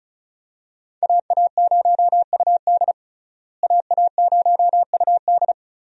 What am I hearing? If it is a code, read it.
Morse code, heard as a keyed tone, about 36 wpm, 700 Hz. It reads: AA0UD AA0UD
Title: USA Call Signs - 5 Characters - Repeated 3x - 35wpm